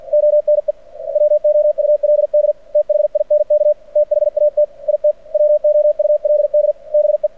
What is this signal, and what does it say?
Morse code, heard as a keyed tone, about 49 wpm, 600 Hz. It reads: ONE WOULD THINK THAT IT WOULD BE
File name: archived / CRSnet